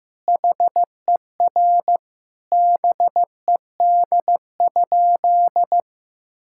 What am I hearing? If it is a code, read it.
Morse code, heard as a keyed tone, about 15 wpm, 700 Hz. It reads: HER BED?
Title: Sentences from Top 1000 Words - Review Part 6 - 15wpm